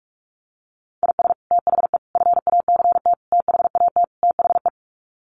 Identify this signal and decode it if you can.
Morse code, heard as a keyed tone, about 44 wpm, 690 Hz. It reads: IS THE FACT THAT THE
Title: Sets of 5 Words Encore2 45wpm